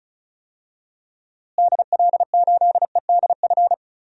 Morse code, heard as a keyed tone, about 34 wpm, 700 Hz. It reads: DL8EDF
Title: DXCC-100 Call Signs - 6 Characters - Repeated 3x - 35wpm